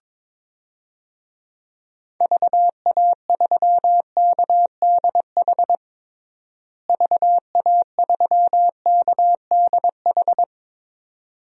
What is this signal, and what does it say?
Morse code, heard as a keyed tone, about 22 wpm, 700 Hz. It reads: VA3KDH VA3KDH
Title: DXCC-100 Call Signs - 6 Characters - Repeated 2x - 22wpm